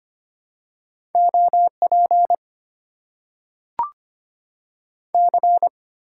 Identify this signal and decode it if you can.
Morse code, heard as a keyed tone, about 25 wpm, 700 Hz. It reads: OP C
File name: Top 90 QSO Elements - 6x Character Spacing - 25wpm